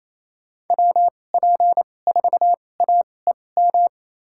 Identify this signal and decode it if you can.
Morse code, heard as a keyed tone, about 28 wpm, spaced as for 20 wpm, 700 Hz. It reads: WP4AEM
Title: USA Call Signs - 6 Characters - 2x Character Spacing - 28wpm